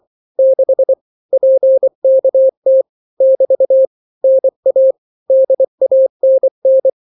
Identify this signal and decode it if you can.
Morse code, heard as a keyed tone, about 24 wpm, 530 Hz. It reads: 6 PKT <BT> NA DANN
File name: daily / 23wpm